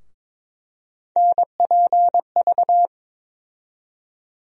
Morse code, heard as a keyed tone, about 22 wpm, 700 Hz. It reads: NPV
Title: USA Call Sign Postfixes - 3 Letters 22wpm